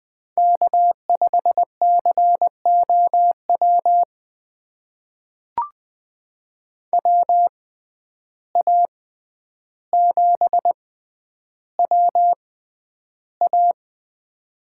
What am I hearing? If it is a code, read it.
Morse code, heard as a keyed tone, about 20 wpm, spaced as for 10 wpm, 700 Hz. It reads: K5COW WA7WA
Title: USA Call Signs - 5 Characters - 6x Character Spacing - 20wpm